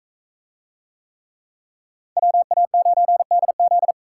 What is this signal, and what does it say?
Morse code, heard as a keyed tone, about 44 wpm, 700 Hz. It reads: WA9DZ